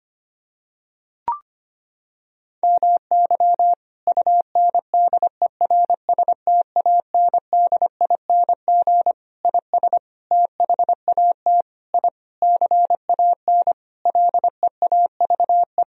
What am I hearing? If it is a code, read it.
Morse code, heard as a keyed tone, about 25 wpm, 700 Hz. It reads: MY UNDERSTANDING IS THAT I CAN LEAVE
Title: Sentences from Top 1000 Words - Review Part 3 - 25wpm